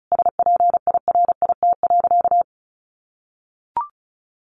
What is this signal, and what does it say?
Morse code, heard as a keyed tone, about 34 wpm, 700 Hz. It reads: SPIRIT.
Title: Sentences from Top 1400 Words 35wpm